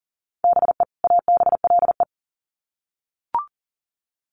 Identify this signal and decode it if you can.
Morse code, heard as a keyed tone, about 40 wpm, 700 Hz. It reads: BE ABLE
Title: Sets of 2 Words 40wpm